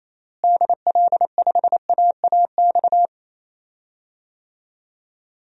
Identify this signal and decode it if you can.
Morse code, heard as a keyed tone, about 28 wpm, 700 Hz. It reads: DL5AAX
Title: DXCC-100 Call Signs - 6 Characters - Repeated 2x - 28wpm